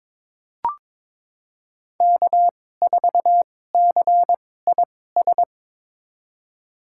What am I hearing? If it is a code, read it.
Morse code, heard as a keyed tone, about 22 wpm, spaced as for 16 wpm, 700 Hz. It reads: K4CIS
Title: USA Call Signs - 2x Character Spacing - 22wpm